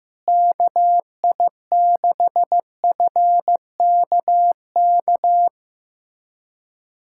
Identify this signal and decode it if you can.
Morse code, heard as a keyed tone, about 15 wpm, 700 Hz. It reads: KI6FKK
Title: USA Callsign DE Callsign - 6 Characters - 15wpm